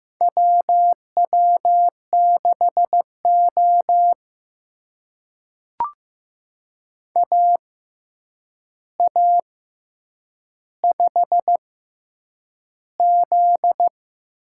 Morse code, heard as a keyed tone, about 15 wpm, spaced as for 8 wpm, 700 Hz. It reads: WW6O AA5Z
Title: USA Call Signs - 4 Characters - 6x Character Spacing - 15wpm